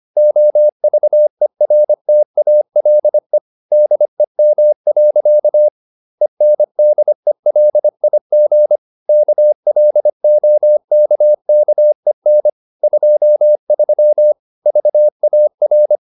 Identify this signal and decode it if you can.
Morse code, heard as a keyed tone, about 25 wpm, 600 Hz. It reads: OVERTALE DEM. ENDELIG KLOKKEN 23 VAR
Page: Ferie 25 wpm